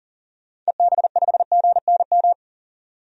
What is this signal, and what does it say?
Morse code, heard as a keyed tone, about 40 wpm, 700 Hz. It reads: EB5GNM
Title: DXCC-100 Call Signs - 6 Characters - 40wpm